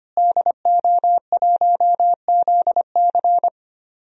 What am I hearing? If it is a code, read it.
Morse code, heard as a keyed tone, about 25 wpm, 700 Hz. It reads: DO1ZC
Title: DXCC-100 Call Signs - 5 Characters - Repeated 3x - 25wpm